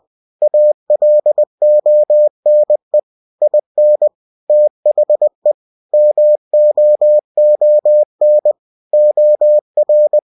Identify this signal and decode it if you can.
Morse code, heard as a keyed tone, about 20 wpm, 600 Hz. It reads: ALONE IN THE MOON OR